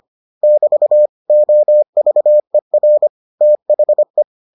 Morse code, heard as a keyed tone, about 25 wpm, 600 Hz. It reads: <BT> OVER THE